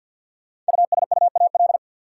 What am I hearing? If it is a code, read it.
Morse code, heard as a keyed tone, about 52 wpm, 700 Hz. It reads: USUAL
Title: Sentences from Top 400 Words - New Word Drill - 50wpm